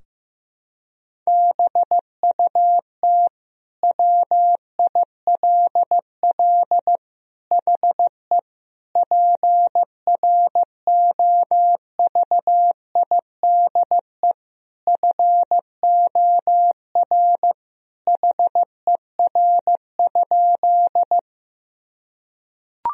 Morse code, heard as a keyed tone, about 16 wpm, 700 Hz. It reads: BUT WILL HE PROVIDE FOR HER?
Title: Sentences from Top 1000 Words - Review Part 3 - 15wpm